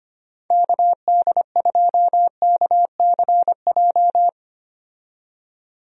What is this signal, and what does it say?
Morse code, heard as a keyed tone, about 25 wpm, 700 Hz. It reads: KD2KCJ
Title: USA Call Signs - 6 Characters - 25wpm